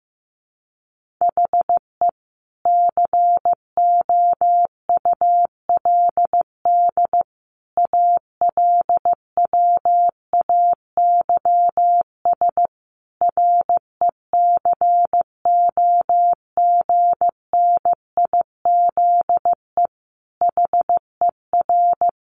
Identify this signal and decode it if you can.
Morse code, heard as a keyed tone, about 15 wpm, 700 Hz. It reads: HE COULD ALWAYS RECOGNIZE HER
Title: Sentences from Top 1400 Words 15wpm